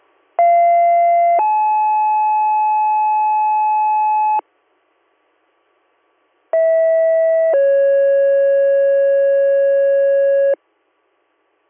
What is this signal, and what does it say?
Two-tone page, heard as a one-second tone, then a three-second tone, about 690 then 860 Hz, about 650 then 550 Hz. It is heard from 0.4 to 10.5 s.